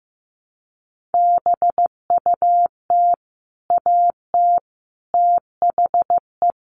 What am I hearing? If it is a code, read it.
Morse code, heard as a keyed tone, about 15 wpm, 700 Hz. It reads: BUT AT THE
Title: Sets of 5 Words Encore2 15wpm